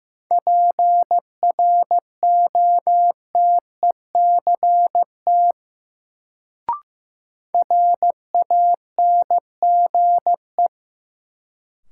Morse code, heard as a keyed tone, about 15 wpm, 700 Hz. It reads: PROTECT RANGE
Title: Sentences from Top 900 Words - New Word Drill - 15wpm